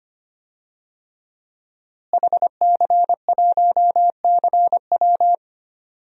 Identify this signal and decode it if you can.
Morse code, heard as a keyed tone, about 25 wpm, 700 Hz. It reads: HC1CW
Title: DXCC-100 Call Signs - 5 Characters - Repeated 3x - 25wpm